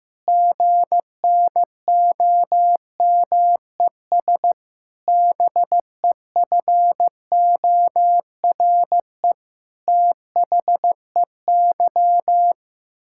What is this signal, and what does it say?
Morse code, heard as a keyed tone, about 15 wpm, 700 Hz. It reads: GNOMES BEFORE THEY
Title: Pangram Challenge v2 - 15wpm